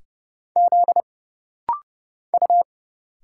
Morse code, heard as a keyed tone, about 29 wpm, 700 Hz. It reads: Z U
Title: Single Letters - Rapid-Fire 30wpm